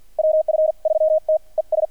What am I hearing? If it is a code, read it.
Morse code, heard as a keyed tone, about 48 wpm, spaced as for 41 wpm, 610 Hz. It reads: WF3T ES
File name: archived / CRSnet